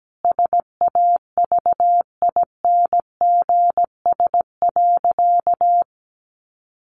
Morse code, heard as a keyed tone, about 17 wpm, 700 Hz. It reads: SAVINGS.
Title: Sentences from Top 1400 Words 17wpm